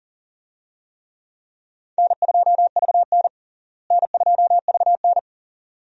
Morse code, heard as a keyed tone, about 37 wpm, 700 Hz. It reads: N2VN N2VN
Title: USA Call Signs - 4 Characters - Repeated 2x - 40wpm